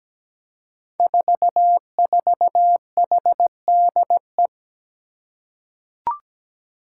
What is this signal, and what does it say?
Morse code, heard as a keyed tone, about 17 wpm, 700 Hz. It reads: 44HDE
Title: Five Challenging Letter-Number Combinations 17wpm